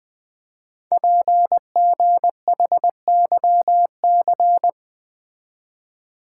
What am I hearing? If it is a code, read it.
Morse code, heard as a keyed tone, about 20 wpm, 700 Hz. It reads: PGHYC
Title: Code Groups - 5 Letters - 20wpm